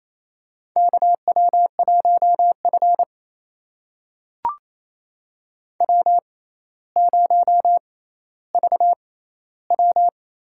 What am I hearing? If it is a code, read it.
Morse code, heard as a keyed tone, about 28 wpm, spaced as for 14 wpm, 700 Hz. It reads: KW1F W0VW
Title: USA Call Signs - 4 Characters - 6x Character Spacing - 28wpm